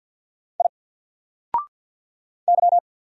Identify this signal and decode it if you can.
Morse code, heard as a keyed tone, about 48 wpm, 700 Hz. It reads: I Y
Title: Single Character (Letter:Number:Punctuation:Prosign) Alt 50wpm